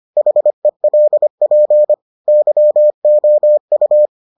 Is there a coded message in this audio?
Yes: Morse code from 0.2 to 4.1 s, heard as a keyed tone, about 25 wpm, 600 Hz. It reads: HELP YOU